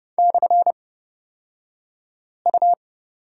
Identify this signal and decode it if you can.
Morse code, heard as a keyed tone, about 30 wpm, spaced as for 5 wpm, 700 Hz. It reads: /U
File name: Lesson 38d - Letter q - Two Characters 30wpm